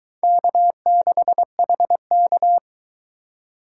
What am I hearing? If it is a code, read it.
Morse code, heard as a keyed tone, about 22 wpm, 700 Hz. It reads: K6HK